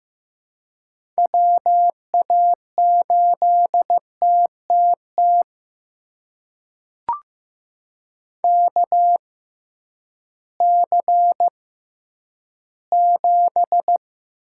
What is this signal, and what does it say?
Morse code, heard as a keyed tone, about 15 wpm, spaced as for 10 wpm, 700 Hz. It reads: WA8TTT KC7
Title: USA Call Signs - 6 Characters - 6x Character Spacing - 15wpm